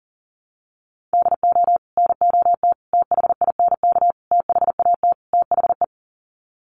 Morse code, heard as a keyed tone, about 40 wpm, 700 Hz. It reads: DO NOT THINK THAT THE
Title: Sets of 5 Words Encore2 40wpm